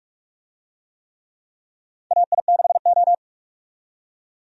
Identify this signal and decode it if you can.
Morse code, heard as a keyed tone, about 47 wpm, 700 Hz. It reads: AI6O
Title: USA Call Signs - 4 Characters - 45wpm